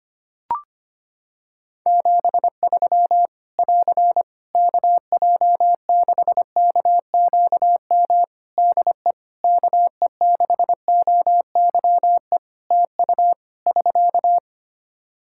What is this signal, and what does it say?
Morse code, heard as a keyed tone, about 25 wpm, 700 Hz. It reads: 73 <AR> KJ6KQM DE KE6OYE TU <SK>
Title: Practice QSO - Exchange 4 - Ending - 25wpm